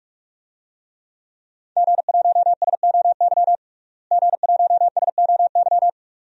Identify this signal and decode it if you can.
Morse code, heard as a keyed tone, about 43 wpm, 700 Hz. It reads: G1SOY G1SOY
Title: DXCC-100 Call Signs - 5 Characters - Repeated 2x - 45wpm